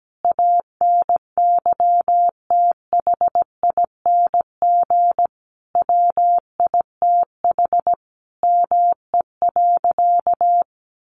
Morse code, heard as a keyed tone, about 17 wpm, 700 Hz. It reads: ANYTHING WITH ME.
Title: Sentences from Top 1000 Words 17wpm